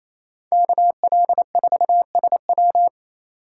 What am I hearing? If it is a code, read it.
Morse code, heard as a keyed tone, about 36 wpm, 700 Hz. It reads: KL4SW